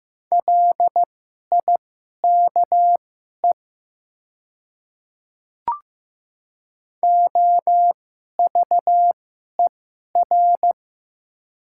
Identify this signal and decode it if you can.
Morse code, heard as a keyed tone, about 15 wpm, spaced as for 11 wpm, 700 Hz. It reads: LIKE OVER
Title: 2x Character Spacing - Top 100 Words - 15wpm